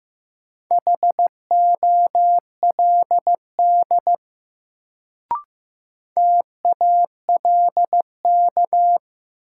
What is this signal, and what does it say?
Morse code, heard as a keyed tone, about 15 wpm, 700 Hz. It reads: HOLD TALK
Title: Sentences from Top 400 Words - New Word Drill - 15wpm